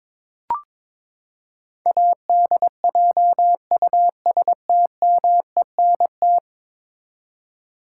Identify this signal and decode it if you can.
Morse code, heard as a keyed tone, about 22 wpm, 700 Hz. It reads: ADJUSTMENT
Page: morse-code-ninja-practice-22wpm